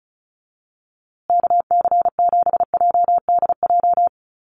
Morse code, heard as a keyed tone, about 34 wpm, 700 Hz. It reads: KC7JDJ
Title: USA Call Signs Repeated 2x 35wpm